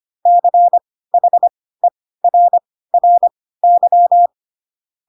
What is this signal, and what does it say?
Morse code, heard as a keyed tone, about 25 wpm, spaced as for 16 wpm, 700 Hz. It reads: CHERRY
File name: Pinocchio / Pinocchio ch1 10 25 ik1hge